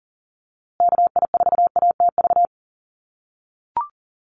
Morse code, heard as a keyed tone, about 40 wpm, 700 Hz. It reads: KI4ATV
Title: USA Call Signs Repeated 3x 40wpm